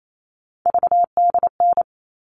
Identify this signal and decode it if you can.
Morse code, heard as a keyed tone, about 28 wpm, 700 Hz. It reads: VDN
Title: All Cut Numbers - 3-Digit 28wpm